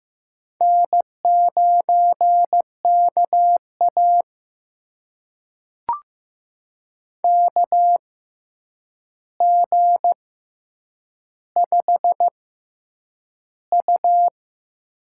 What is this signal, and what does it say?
Morse code, heard as a keyed tone, about 15 wpm, spaced as for 8 wpm, 700 Hz. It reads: N9KA KG5U
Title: USA Call Signs - 4 Characters - 6x Character Spacing - 15wpm